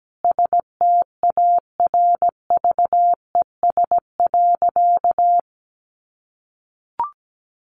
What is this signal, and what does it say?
Morse code, heard as a keyed tone, about 17 wpm, 700 Hz. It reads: STARVES.
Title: A Sound Of Thunder 17wpm